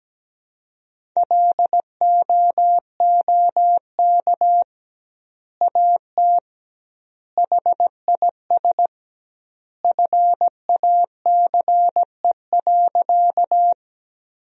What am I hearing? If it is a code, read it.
Morse code, heard as a keyed tone, about 17 wpm, 700 Hz. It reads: LOOK AT HIS FACE.
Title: 2x Word-Spacing - Sentences from Top 300 Words - 17wpm